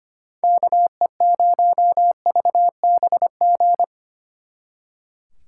Morse code, heard as a keyed tone, about 31 wpm, 700 Hz. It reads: KE0VBG
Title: Speed Racing - USA Call Signs 25wpm